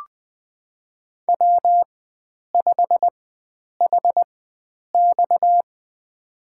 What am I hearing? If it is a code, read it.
Morse code, heard as a keyed tone, about 20 wpm, spaced as for 9 wpm, 700 Hz. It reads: W5HX
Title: USA Call Signs - 4 Characters - 4x Character Spacing - 20wpm